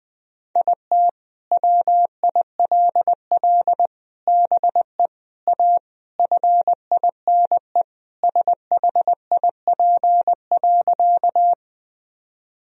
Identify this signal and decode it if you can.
Morse code, heard as a keyed tone, about 19 wpm, 700 Hz. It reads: IT WILL BE A FINE SHIP.
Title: Sentences from Top 1000 Words - Review Part 2 - 20wpm